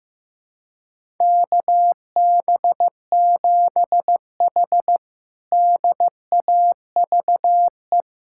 Morse code, heard as a keyed tone, about 15 wpm, 700 Hz. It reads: KB7H DAVE
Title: K1USN SST QSO Practice - 15wpm